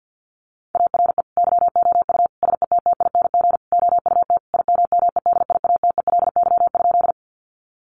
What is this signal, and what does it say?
Morse code, heard as a keyed tone, about 50 wpm, 700 Hz. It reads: ARE YOU SETTING OUT IMMEDIATELY?
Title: Sentences from Top 1300 Words 50wpm